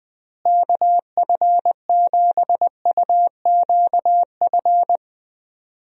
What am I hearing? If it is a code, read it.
Morse code, heard as a keyed tone, about 20 wpm, 700 Hz. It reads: KF7UQF
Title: USA Call Signs - 6 Characters - Repeated 2x - 20wpm